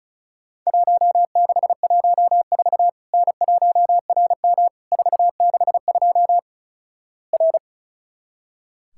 Morse code, heard as a keyed tone, about 36 wpm, 700 Hz. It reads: 1614 N1RM 462 R
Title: Worked All Europe DX Contest - QTC Exchange - 35wpm